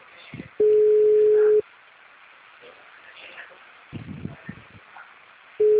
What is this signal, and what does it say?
Ringback tone (Europe, 425 Hz).
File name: out- -2036- - - .90042